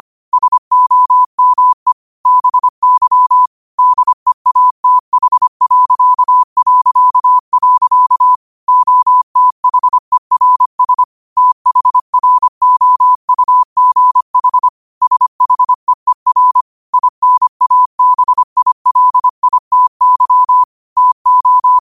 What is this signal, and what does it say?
Morse code, heard as a keyed tone, about 25 wpm, 1000 Hz. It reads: SOME BY DEATH... OTHERS THROUGH SHEER INABILITY TO
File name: qotdfiles / 25wpm